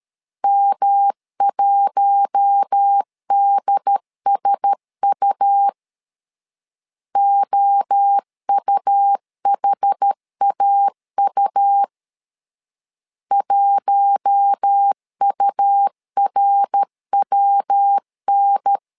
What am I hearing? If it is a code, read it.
Morse code, heard as a keyed tone, about 13 wpm, 800 Hz. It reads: M1DSU OUHAU 1URWN
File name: LEZIONI / 4 Q1S 12wpm 1802